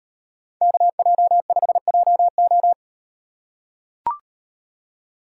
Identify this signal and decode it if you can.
Morse code, heard as a keyed tone, about 34 wpm, 700 Hz. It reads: KJ5JO